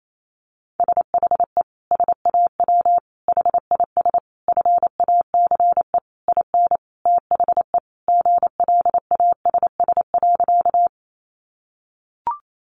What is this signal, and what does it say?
Morse code, heard as a keyed tone, about 28 wpm, 700 Hz. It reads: SHE SAW HIS FACE IN THE GLASS.
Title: Sentences from Top 1300 Words 28wpm